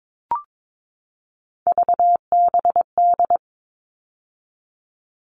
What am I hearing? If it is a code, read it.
Morse code, heard as a keyed tone, about 22 wpm, 700 Hz. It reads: VBD
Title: All Cut Numbers - 3-Digit 22wpm